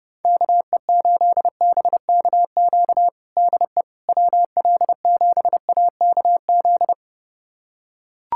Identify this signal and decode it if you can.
Morse code, heard as a keyed tone, about 32 wpm, 690 Hz. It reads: KE8BKQ DE WL7AKZ
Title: USA Callsign DE Callsign - 6 Characters - 30wpm